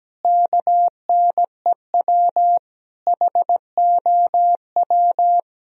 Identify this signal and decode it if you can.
Morse code, heard as a keyed tone, about 17 wpm, 700 Hz. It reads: KNEW HOW
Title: Sentences from Top 1000 Words - Review Part 4 - 17wpm